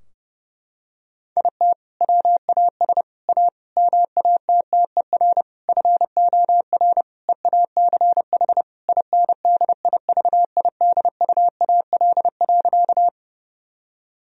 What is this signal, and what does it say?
Morse code, heard as a keyed tone, about 30 wpm, 700 Hz. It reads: IT WAS A MATTER FOR EACH INDIVIDUAL.
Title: Sentences from Top 1000 Words - Review Part 4 - 30wpm